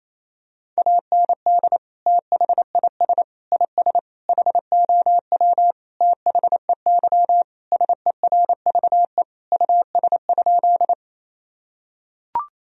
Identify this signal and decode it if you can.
Morse code, heard as a keyed tone, about 28 wpm, 700 Hz. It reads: AND THIS IS HOW THEY SERVE US?
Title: Sentences from Top 1000 Words - Review Part 2 - 28wpm